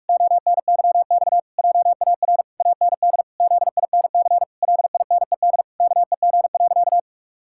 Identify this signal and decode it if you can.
Morse code, heard as a keyed tone, about 46 wpm, 700 Hz. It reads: ONYX JAR AND ZINC LINED KEG.
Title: Pangram Challenge v2 - 45wpm